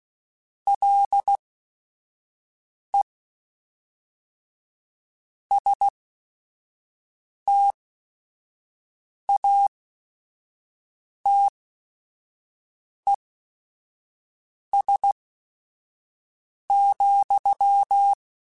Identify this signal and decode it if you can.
Morse code, heard as a keyed tone, about 16 wpm, spaced as for 5 wpm, 800 Hz. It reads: LE STATES,
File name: Technical Subjects / Technical Subjects-mp3s / AND Gate @5